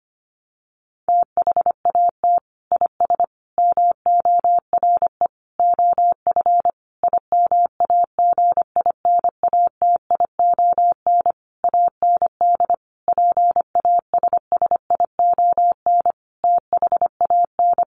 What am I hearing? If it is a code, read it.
Morse code, heard as a keyed tone, about 25 wpm, 700 Hz. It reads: THAT IS MORE OF IMAGINATION AND PASSION THAN
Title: Sentences from Top 1300 Words 25wpm